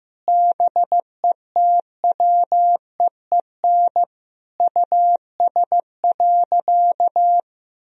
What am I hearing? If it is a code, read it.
Morse code, heard as a keyed tone, about 15 wpm, 700 Hz. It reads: BETWEEN US.
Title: Sentences from Top 1000 Words - Review Part 1 - 15wpm